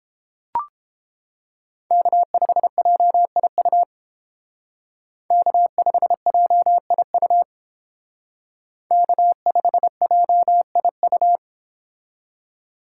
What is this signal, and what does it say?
Morse code, heard as a keyed tone, about 30 wpm, 700 Hz. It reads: K5JIU K5JIU K5JIU